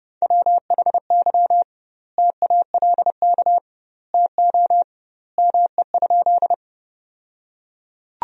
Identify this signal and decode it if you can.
Morse code, heard as a keyed tone, about 30 wpm, 700 Hz. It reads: WHY TALK TO ME?
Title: 2x Word-Spacing - Sentences from Top 400 Words - 30wpm